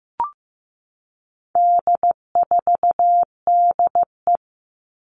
Morse code, heard as a keyed tone, about 15 wpm, 700 Hz. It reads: D4DE
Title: All Cut Numbers - 4-Digit 15wpm